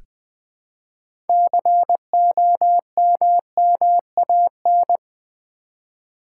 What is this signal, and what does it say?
Morse code, heard as a keyed tone, about 20 wpm, 700 Hz. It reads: COMMAN
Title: Build A Word - 3 Letters At A Time - 9-Letter Words - 20wpm